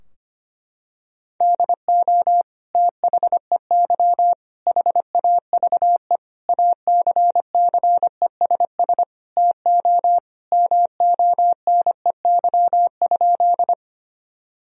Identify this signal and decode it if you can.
Morse code, heard as a keyed tone, about 25 wpm, 700 Hz. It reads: DO THEY HAVE ACCESS TO MONEY?
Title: Sentences from Top 1000 Words - Review Part 4 - 25wpm